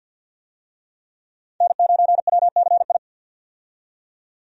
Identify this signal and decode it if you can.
Morse code, heard as a keyed tone, about 48 wpm, 700 Hz. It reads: N9WCI
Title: USA Call Signs - 5 Characters - 50wpm